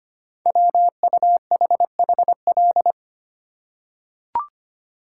Morse code, heard as a keyed tone, about 25 wpm, 700 Hz. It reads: WUHHL
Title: Code Groups - 5 Letters - 25wpm